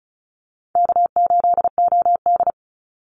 Morse code, heard as a keyed tone, about 34 wpm, 700 Hz. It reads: K8OD
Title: Call Signs - 35wpm